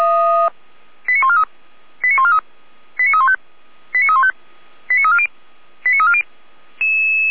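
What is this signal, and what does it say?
Five-tone selective call (ZVEI), heard as quick runs of five tones, tones of about 80 ms. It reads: 89143 89143 89315 89315 89340 89340
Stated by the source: ZVEI five-tone call 89167 received on an SDR, archived by feuerwehr-orlamuende.org fire brigade.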